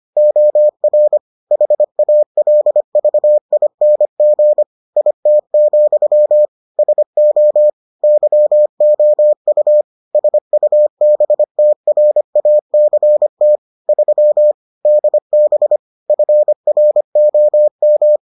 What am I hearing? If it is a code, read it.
Morse code, heard as a keyed tone, about 25 wpm, 600 Hz. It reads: OR HALVING IT, SO YOU SUBTRACT 3 DB FROM